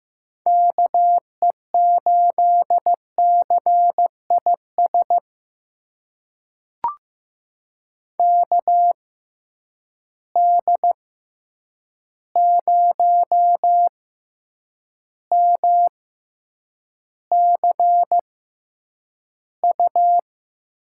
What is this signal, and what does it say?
Morse code, heard as a keyed tone, about 15 wpm, spaced as for 8 wpm, 700 Hz. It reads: KE8CIS KD0MCU
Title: USA Call Signs - 6 Characters - 6x Character Spacing - 15wpm